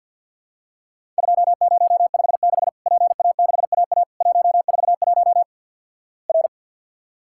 Morse code, heard as a keyed tone, about 50 wpm, 700 Hz. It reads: 2056 PA6AA 141 R
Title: Worked All Europe DX Contest - QTC Exchange - 50wpm